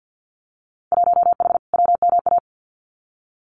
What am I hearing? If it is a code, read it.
Morse code, heard as a keyed tone, about 50 wpm, 700 Hz. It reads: 1H WMA
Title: Winter Field Day Exchanges 50wpm